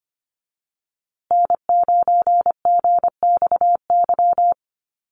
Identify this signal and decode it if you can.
Morse code, heard as a keyed tone, about 25 wpm, 700 Hz. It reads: N9GXY
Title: USA Call Signs Repeated 3x 25wpm